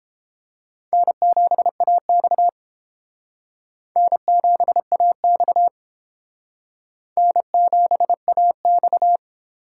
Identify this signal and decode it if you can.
Morse code, heard as a keyed tone, about 30 wpm, 700 Hz. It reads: N7AX N7AX N7AX